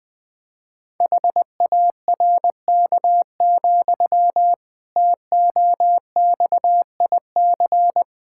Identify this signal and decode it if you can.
Morse code, heard as a keyed tone, about 20 wpm, 700 Hz. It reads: HARK, TOXIC
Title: Pangram Challenge v2 - 20wpm